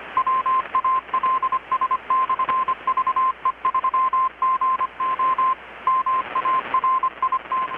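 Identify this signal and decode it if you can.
Morse code, heard as a keyed tone, about 25 wpm, 1040 Hz, with radio noise. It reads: WALS/VE3GO MARIT
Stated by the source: Morse from KPH's Night of Nights XX maritime radiotelegraphy event, received in Chiapas, Mexico about 3,800 km away on a Kenwood R-600 with a grounded 10 m long-wire antenna.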